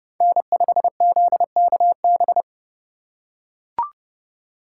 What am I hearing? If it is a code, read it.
Morse code, heard as a keyed tone, about 30 wpm, 700 Hz. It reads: N5ZKB
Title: USA Call Signs - 5 Characters - 2x Character Spacing - 30wpm